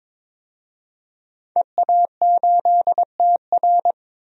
Morse code, heard as a keyed tone, about 22 wpm, 700 Hz. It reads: EA8TR
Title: DXCC-100 Call Signs - 5 Characters - 22wpm